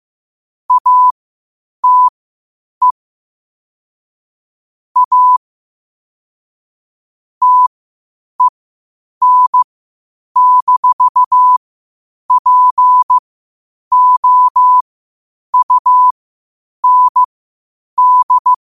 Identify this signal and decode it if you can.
Morse code, heard as a keyed tone, about 15 wpm, spaced as for 8 wpm, 1000 Hz. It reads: ATE A TEN-POUND
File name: qotdfiles / 8wpm